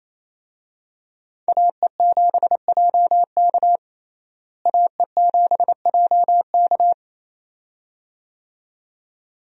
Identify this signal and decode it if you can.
Morse code, heard as a keyed tone, about 28 wpm, 700 Hz. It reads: AE7JK AE7JK
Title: USA Call Signs - 5 Characters - Repeated 2x - 28wpm